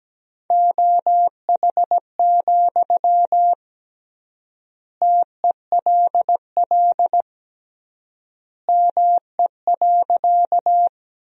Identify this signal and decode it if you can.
Morse code, heard as a keyed tone, about 17 wpm, 700 Hz. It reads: OH, TELL ME.
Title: 3x Word-Spacing - Sentences from Top 500 Words - 17wpm